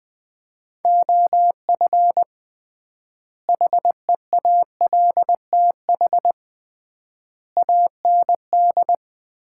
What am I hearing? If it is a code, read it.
Morse code, heard as a keyed tone, about 20 wpm, 700 Hz. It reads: OF HEALTH AND
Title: 3x Word-Spacing - Sets of 5 Words - 20wpm